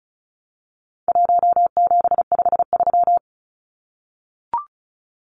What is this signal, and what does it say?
Morse code, heard as a keyed tone, about 34 wpm, 700 Hz. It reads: 1753
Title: Numbers - 4-Digit 35wpm